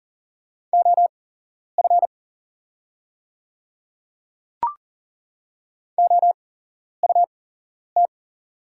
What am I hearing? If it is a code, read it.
Morse code, heard as a keyed tone, about 40 wpm, spaced as for 11 wpm, 700 Hz. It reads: OF OUT
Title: 8x Character Spacing - Top 100 Words - 40wpm